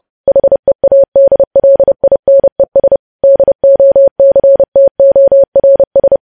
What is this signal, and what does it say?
Morse code, heard as a keyed tone, about 30 wpm, 550 Hz. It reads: HEADLINES DOCTORS